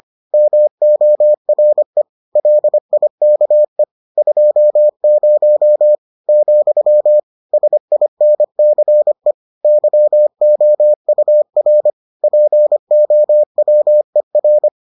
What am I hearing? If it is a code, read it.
Morse code, heard as a keyed tone, about 25 wpm, 600 Hz. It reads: MORE LIKE 20 , SINCE YOUR POWER